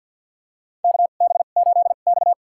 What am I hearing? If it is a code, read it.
Morse code, heard as a keyed tone, about 47 wpm, spaced as for 35 wpm, 700 Hz. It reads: KB8X
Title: USA Call Signs - 4 Characters - 2x Character Spacing - 50wpm